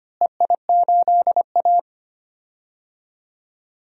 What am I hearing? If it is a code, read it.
Morse code, heard as a keyed tone, about 25 wpm, 700 Hz. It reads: EI8A
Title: DXCC-100 Call Signs - 4 Characters - Repeated 2x - 25wpm